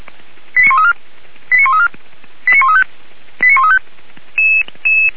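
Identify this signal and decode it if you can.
Five-tone selective call (ZVEI), heard as quick runs of five tones, tones of about 80 ms. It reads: 89134 89134 89135 89135
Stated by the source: ZVEI five-tone call 89132 received by SDR at Feuerwehr Orlamünde (f85360).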